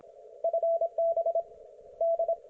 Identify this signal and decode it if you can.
Morse code, heard as a keyed tone, about 26 wpm, 670 Hz, with radio noise. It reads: FB D